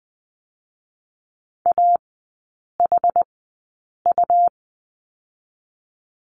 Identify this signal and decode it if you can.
Morse code, heard as a keyed tone, about 20 wpm, 700 Hz. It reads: A H U
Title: 107 Lesson 18e - Letter m - Three Characters 20wpm